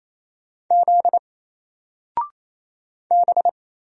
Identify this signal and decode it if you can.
Morse code, heard as a keyed tone, about 28 wpm, 700 Hz. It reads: Z B
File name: Single-Letter-Number-28wpm